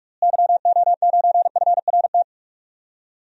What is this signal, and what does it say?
Morse code, heard as a keyed tone, about 46 wpm, 700 Hz. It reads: YO9FRT
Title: DXCC-100 Call Signs - 6 Characters - Repeated 3x - 45wpm